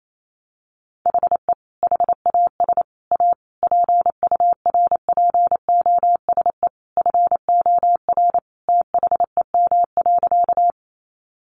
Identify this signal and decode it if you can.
Morse code, heard as a keyed tone, about 28 wpm, 700 Hz. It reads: HE HAS A PURPOSE FOR THEM.
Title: Sentences from Top 800 Words 28wpm